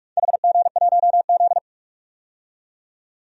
Morse code, heard as a keyed tone, about 43 wpm, 700 Hz. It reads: HG1Z
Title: DXCC-100 Call Signs - 4 Characters - 45wpm